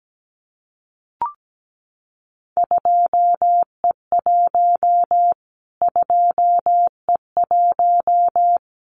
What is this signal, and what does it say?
Morse code, heard as a keyed tone, about 17 wpm, 700 Hz. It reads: 2E1 2E1
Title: Callsign Prefixes for Top 25 DXCC - Repeated 3x - 17wpm